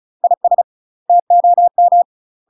Morse code, heard as a keyed tone, about 36 wpm, 700 Hz. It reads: IS TOM
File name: UK MASTER QSOs TRAINING / 35WPM / UK MASTER QSOs 0013